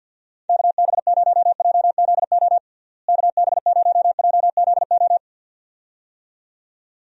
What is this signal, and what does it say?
Morse code, heard as a keyed tone, about 49 wpm, 700 Hz. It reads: KB0JZO KB0JZO
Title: USA Call Signs - 6 Characters - Repeated 2x - 50wpm